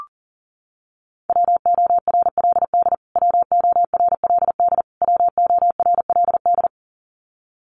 Morse code, heard as a keyed tone, about 40 wpm, 700 Hz. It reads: WORLD WORLD WORLD
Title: Top 300 Words Repeated 40wpm